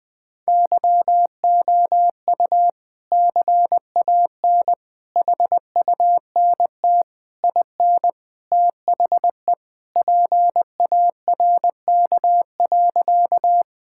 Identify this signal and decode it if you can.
Morse code, heard as a keyed tone, about 20 wpm, 700 Hz. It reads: YOU CAN HUNT IN THE PARK.
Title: Sentences from Top 1600 Words 20wpm